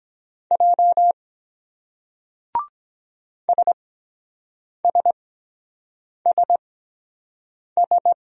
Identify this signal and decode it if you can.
Morse code, heard as a keyed tone, about 23 wpm, spaced as for 7 wpm, 700 Hz. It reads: J SSSS